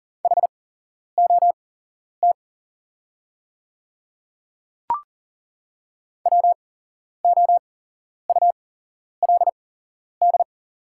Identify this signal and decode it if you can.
Morse code, heard as a keyed tone, about 38 wpm, spaced as for 11 wpm, 690 Hz. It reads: HOT WOULD
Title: 8x Character Spacing - Top 100 Words - 40wpm